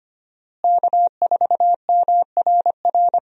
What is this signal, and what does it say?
Morse code, heard as a keyed tone, about 25 wpm, 700 Hz. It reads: K4MRR
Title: USA Call Signs - 5 Characters - Repeated 3x - 25wpm